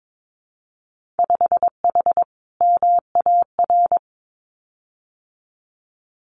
Morse code, heard as a keyed tone, about 22 wpm, 700 Hz. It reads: 5H MAR
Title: Winter Field Day Exchanges Encore 22wpm